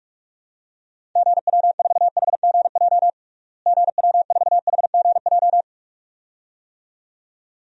Morse code, heard as a keyed tone, about 46 wpm, 700 Hz. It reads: GW4HGJ GW4HGJ
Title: DXCC-100 Call Signs - 6 Characters - Repeated 2x - 45wpm